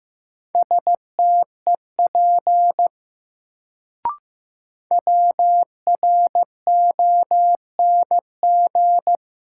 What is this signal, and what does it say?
Morse code, heard as a keyed tone, about 15 wpm, 700 Hz. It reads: STEP WRONG
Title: Sets of 5 Words Encore - New Word Drill - 15wpm